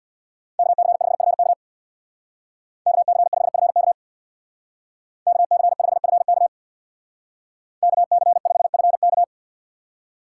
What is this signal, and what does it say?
Morse code, heard as a keyed tone, about 65 wpm, 700 Hz. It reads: KC5FK KC5FK KC5FK KC5FK